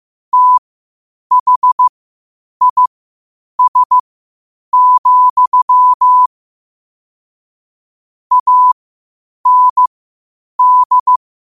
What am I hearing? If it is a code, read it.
Morse code, heard as a keyed tone, about 15 wpm, spaced as for 8 wpm, 1000 Hz. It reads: THIS, AND
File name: qotdfiles / 8wpm